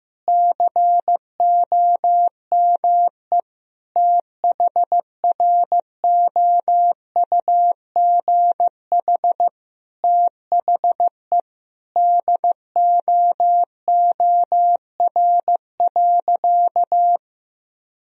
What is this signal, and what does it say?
Morse code, heard as a keyed tone, about 15 wpm, 700 Hz. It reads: COME THROUGH THE DOOR.
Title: Sentences from Top 1000 Words - Review Part 1 - 15wpm